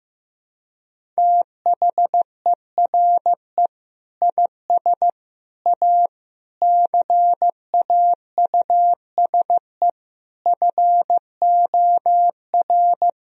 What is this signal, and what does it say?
Morse code, heard as a keyed tone, about 15 wpm, 700 Hz. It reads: THERE IS A CAUSE FOR
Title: Sentences from Top 1000 Words - Review Part 1 - 15wpm